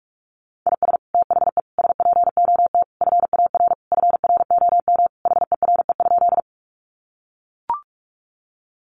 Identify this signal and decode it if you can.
Morse code, heard as a keyed tone, about 44 wpm, 700 Hz. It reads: IS THE SPOT FAR FROM HERE?
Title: Sentences from Top 1100 Words 45wpm